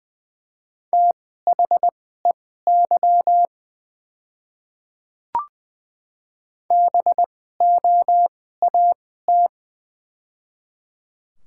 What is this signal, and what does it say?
Morse code, heard as a keyed tone, about 21 wpm, spaced as for 15 wpm, 700 Hz. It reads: THEY BOAT
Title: 2x Character Spacing - Top 500 Words - 20wpm